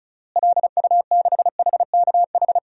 Morse code, heard as a keyed tone, about 36 wpm, 700 Hz. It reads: LU6HKH
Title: DXCC-100 Call Signs - 6 Characters - Repeated 3x - 35wpm